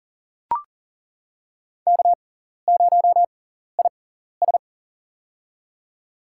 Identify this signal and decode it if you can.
Morse code, heard as a keyed tone, about 39 wpm, spaced as for 14 wpm, 700 Hz. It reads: K0IS
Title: USA Call Signs - 4 Characters - 6x Character Spacing - 40wpm